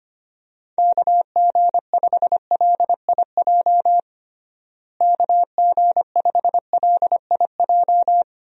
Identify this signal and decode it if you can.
Morse code, heard as a keyed tone, about 25 wpm, 700 Hz. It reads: KG5LIJ KG5LIJ
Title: USA Call Signs - 6 Characters - Repeated 3x - 25wpm